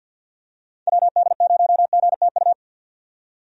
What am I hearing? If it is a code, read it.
Morse code, heard as a keyed tone, about 51 wpm, 700 Hz. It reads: WD0GTU
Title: USA Call Signs - 6 Characters - 50wpm